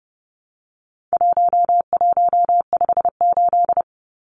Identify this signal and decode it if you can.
Morse code, heard as a keyed tone, about 30 wpm, 700 Hz. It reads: 1158
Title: Numbers - 4-Digit 30wpm